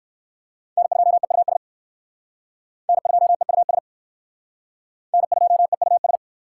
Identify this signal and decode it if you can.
Morse code, heard as a keyed tone, about 63 wpm, 700 Hz. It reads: N2EUS N2EUS N2EUS